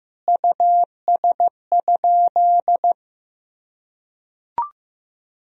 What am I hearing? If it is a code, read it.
Morse code, heard as a keyed tone, about 16 wpm, 700 Hz. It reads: US?
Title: Sentences from Top 1000 Words - Review Part 2 - 15wpm